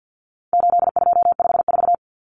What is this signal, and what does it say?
Morse code, heard as a keyed tone, about 50 wpm, 700 Hz. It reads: 8254
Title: Common Cut Numbers - 4-Digit 50wpm